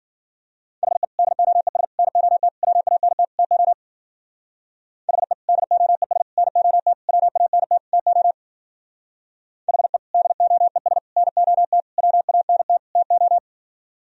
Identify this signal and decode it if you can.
Morse code, heard as a keyed tone, about 50 wpm, 700 Hz. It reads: HE DOES NOT WANT TO HE DOES NOT WANT TO HE DOES NOT WANT TO
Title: Speed-Racing - Sets of 5 Words 40wpm